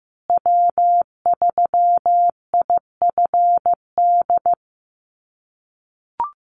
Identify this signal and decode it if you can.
Morse code, heard as a keyed tone, about 15 wpm, 700 Hz. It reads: W3IFD
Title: USA Call Signs Repeated 2x 15wpm